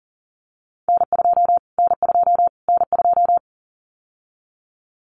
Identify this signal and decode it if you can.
Morse code, heard as a keyed tone, about 40 wpm, 700 Hz. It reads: N2 N2 N2
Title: Callsign Prefixes for Top 25 DXCC - Repeated 3x - 40wpm